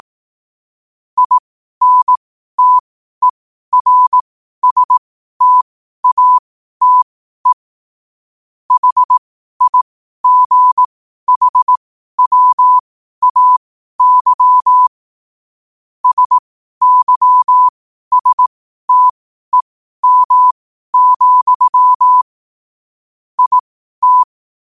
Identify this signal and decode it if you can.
Morse code, heard as a keyed tone, about 18 wpm, spaced as for 12 wpm, 1000 Hz. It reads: INTERSTATE HIGHWAY SYSTEM, IT
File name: qotdfiles / 12wpm